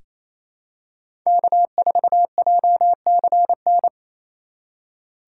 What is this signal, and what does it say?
Morse code, heard as a keyed tone, about 28 wpm, 700 Hz. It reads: K4JCN
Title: USA Call Signs - 5 Characters - 4x Character Spacing - 28wpm